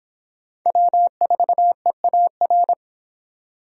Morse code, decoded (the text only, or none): W4EAR